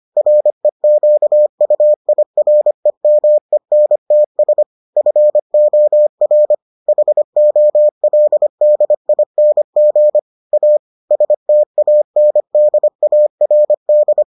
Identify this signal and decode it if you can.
Morse code, heard as a keyed tone, about 25 wpm, 600 Hz. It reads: REQUIREMENTS FOR HOLDING A STANDARD